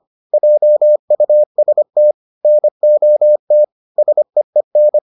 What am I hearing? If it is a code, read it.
Morse code, heard as a keyed tone, about 25 wpm, 600 Hz. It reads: JUST NOT SEEN